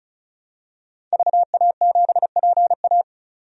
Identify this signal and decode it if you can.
Morse code, heard as a keyed tone, about 36 wpm, 700 Hz. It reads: VA7PA
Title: DXCC-100 Call Signs - 5 Characters - 35wpm